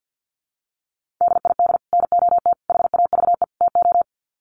Morse code, heard as a keyed tone, about 50 wpm, 700 Hz. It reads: DID NOT HAVE TO
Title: Sets of 4 Words 50wpm